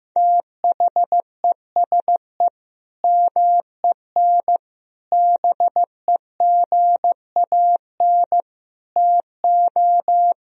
Morse code, heard as a keyed tone, about 15 wpm, 700 Hz. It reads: THESE MEN BEGAN TO
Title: Sentences from Top 1000 Words - Review Part 1 - 15wpm